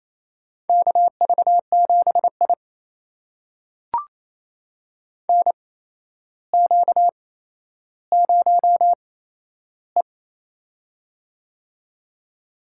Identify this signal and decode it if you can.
Morse code, heard as a keyed tone, about 28 wpm, spaced as for 12 wpm, 700 Hz. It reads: KV7I NQ0E
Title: USA Call Signs - 4 Characters - 8x Character Spacing - 28wpm